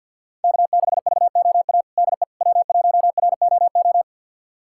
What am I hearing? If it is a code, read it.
Morse code, heard as a keyed tone, about 50 wpm, 700 Hz. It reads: K6VOA DE W1ROO
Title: USA Callsign DE Callsign - 5 Characters - 50wpm